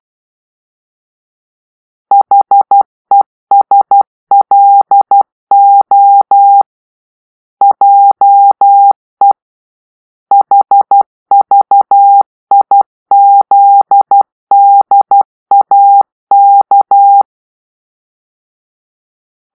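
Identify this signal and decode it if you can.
Morse code, heard as a keyed tone, about 12 wpm, 800 Hz. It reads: HESLO JE HVIZDAK
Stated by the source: Morse-code cipher made by a player team for the Kachna puzzle game.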